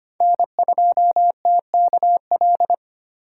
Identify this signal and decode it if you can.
Morse code, heard as a keyed tone, about 25 wpm, 700 Hz. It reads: N2TKL
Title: USA Call Signs - 4x Character Spacing - 25wpm